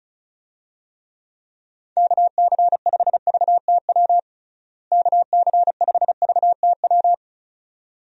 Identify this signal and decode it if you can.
Morse code, heard as a keyed tone, about 34 wpm, 700 Hz. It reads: KC5VTW KC5VTW
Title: USA Call Signs - 6 Characters - Repeated 2x - 35wpm